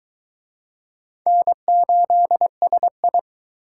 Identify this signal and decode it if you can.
Morse code, heard as a keyed tone, about 23 wpm, 700 Hz. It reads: N8SI